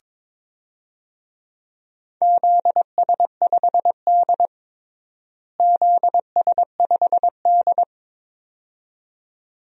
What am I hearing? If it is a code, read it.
Morse code, heard as a keyed tone, about 22 wpm, 700 Hz. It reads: ZS5D ZS5D
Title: DXCC-100 Call Signs - 4 Characters - Repeated 2x - 22wpm